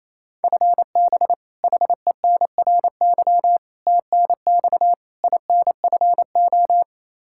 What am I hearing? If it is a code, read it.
Morse code, heard as a keyed tone, about 27 wpm, 700 Hz. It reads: FB HENRY TNX INFO
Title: Practice QSO - Exchange 3 - More Info - 28wpm